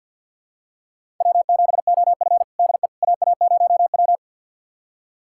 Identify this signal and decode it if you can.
Morse code, heard as a keyed tone, about 50 wpm, 700 Hz. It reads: W7OF DE AA0W
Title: USA Callsign DE Callsign - 4 Characters - 50wpm